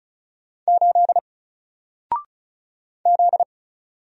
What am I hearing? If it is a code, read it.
Morse code, heard as a keyed tone, about 35 wpm, 700 Hz. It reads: 8 Z
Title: Single Character (Letter:Number:Punctuation:Prosign) Alt 35wpm